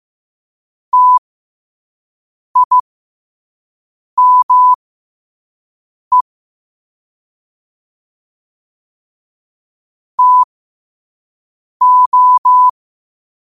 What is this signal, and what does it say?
Morse code, heard as a keyed tone, about 15 wpm, spaced as for 5 wpm, 1000 Hz. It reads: TIME TO
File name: qotdfiles / 5wpm